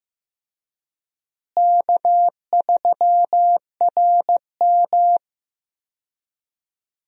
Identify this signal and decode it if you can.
Morse code, heard as a keyed tone, about 15 wpm, 700 Hz. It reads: K3RM
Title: USA Call Signs - 4 Characters - Repeated 2x - 15wpm